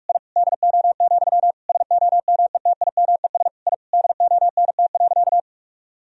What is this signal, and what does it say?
Morse code, heard as a keyed tone, about 46 wpm, 700 Hz. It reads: I DO, SOMETIMES I DONT.
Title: Sentences from Top 1000 Words - Review Part 3 - 45wpm